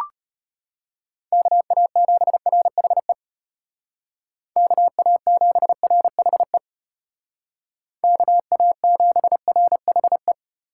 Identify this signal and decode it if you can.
Morse code, heard as a keyed tone, about 30 wpm, 700 Hz. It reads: KA7RHE KA7RHE KA7RHE
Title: Speed Racing - USA 2x3 Call Signs 25wpm